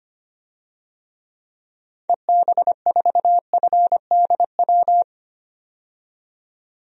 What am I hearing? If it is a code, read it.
Morse code, heard as a keyed tone, about 25 wpm, 700 Hz. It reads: EB4FDW
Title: DXCC-100 Call Signs - 6 Characters - 25wpm